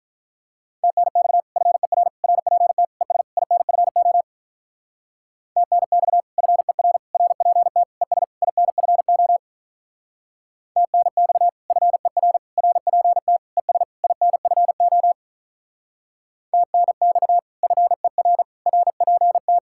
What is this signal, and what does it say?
Morse code, heard as a keyed tone, about 43 wpm, 700 Hz. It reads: TNX FER RPT ES INFO TNX FER RPT ES INFO TNX FER RPT ES INFO TNX FER RPT
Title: Speed-Racing Common QSO Phrases Encore - 35wpm